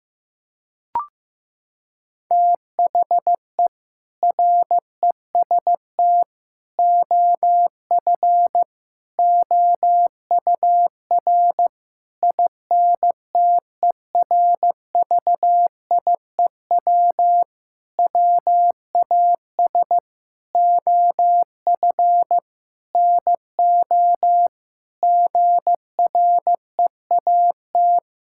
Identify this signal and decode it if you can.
Morse code, heard as a keyed tone, about 15 wpm, 700 Hz. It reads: THE REST OF OUR INTERVIEW WAS OF NO GREAT
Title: Sentences from Top 1600 Words 15wpm